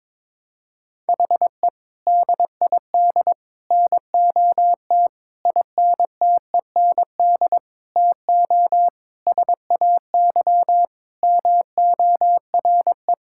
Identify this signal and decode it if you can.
Morse code, heard as a keyed tone, about 22 wpm, 700 Hz. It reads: HE DID NOT INTEND TO SAY MORE
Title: Sentences from Top 1600 Words 22wpm